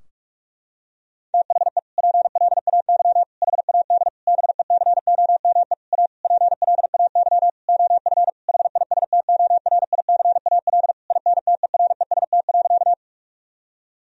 Morse code, heard as a keyed tone, about 41 wpm, 700 Hz. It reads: THE PLAY HAD BECOME A PLAY OF HISTORICAL INTEREST.
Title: Sentences from Top 1000 Words - Review Part 4 - 45wpm